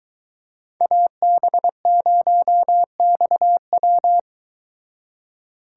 Morse code, heard as a keyed tone, about 22 wpm, 700 Hz. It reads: AB0XW